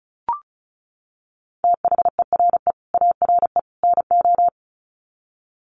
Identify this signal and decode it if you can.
Morse code, heard as a keyed tone, about 34 wpm, 700 Hz. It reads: THERE ARE NO
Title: Sets of 3 Words 35wpm